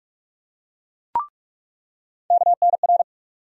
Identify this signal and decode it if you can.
Morse code, heard as a keyed tone, about 46 wpm, 700 Hz. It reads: KNR
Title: USA Call Sign Postfixes - 3 Letters 45wpm